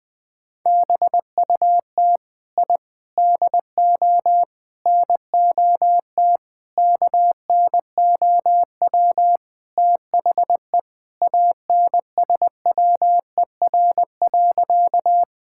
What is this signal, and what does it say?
Morse code, heard as a keyed tone, about 20 wpm, 700 Hz. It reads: BUT I DO NOT KNOW THE ANSWER.
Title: Sentences with 5-Word Phrases - 20wpm